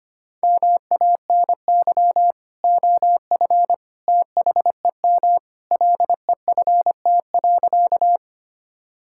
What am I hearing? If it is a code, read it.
Morse code, heard as a keyed tone, about 25 wpm, 700 Hz. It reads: MANY OF THEM LEFT.
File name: practice / 25wpm / Sentences with 2-Word Phrases - 25wpm